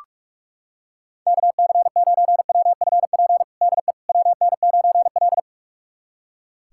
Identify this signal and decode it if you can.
Morse code, heard as a keyed tone, about 45 wpm, 700 Hz. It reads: KC9WFP DE WN9L
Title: USA Callsign DE Callsign 45wpm